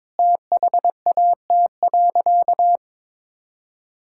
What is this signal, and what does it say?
Morse code, heard as a keyed tone, about 22 wpm, 700 Hz. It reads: THAT.
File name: practice / 22wpm / Sentences with 3-Word Phrases - 22wpm